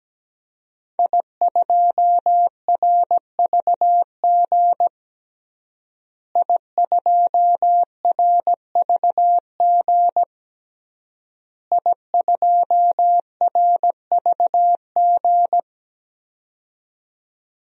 Morse code, heard as a keyed tone, about 17 wpm, 700 Hz. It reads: I2RVG I2RVG I2RVG
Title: DXCC-100 Call Signs - 5 Characters - Repeated 3x - 17wpm